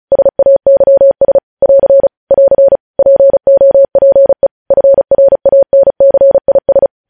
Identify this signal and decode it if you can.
Morse code, heard as a keyed tone, about 36 wpm, 550 Hz. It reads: SAYS <AR> <AR> POPE FRANCIS
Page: News Headlines in Morse Code at 35 WPM